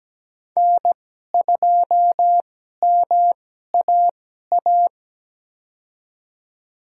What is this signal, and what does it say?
Morse code, heard as a keyed tone, about 17 wpm, spaced as for 12 wpm, 700 Hz. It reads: N2MAA
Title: USA Call Signs - 5 Characters - 2x Character Spacing - 17wpm